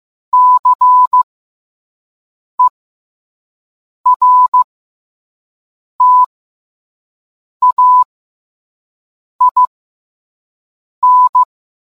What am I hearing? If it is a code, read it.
Morse code, heard as a keyed tone, about 15 wpm, spaced as for 5 wpm, 1000 Hz. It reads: CERTAIN